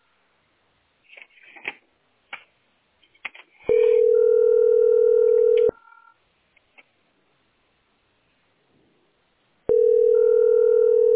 Ringback tone (North America), 440 + 480 Hz.